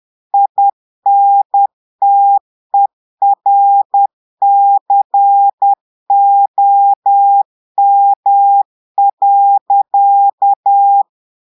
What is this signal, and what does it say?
Morse code, heard as a keyed tone, about 10 wpm, 800 Hz. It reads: INTERCOM.
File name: Portugais - texte en clair 1 - vitesse - 10 mots minute